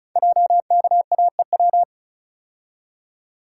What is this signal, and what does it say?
Morse code, heard as a keyed tone, about 36 wpm, 700 Hz. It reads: JKAEW
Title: Code Groups - 5 Letters - 35wpm